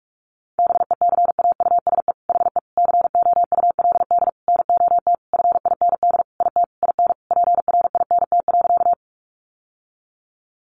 Morse code, heard as a keyed tone, about 44 wpm, 700 Hz. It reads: BECAUSE HE COULD NOT FIND IT IN PRINT.
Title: Sentences from Top 1400 Words 45wpm